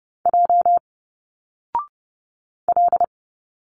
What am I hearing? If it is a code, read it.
Morse code, heard as a keyed tone, about 30 wpm, 700 Hz. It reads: J L
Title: Single Letters 30wpm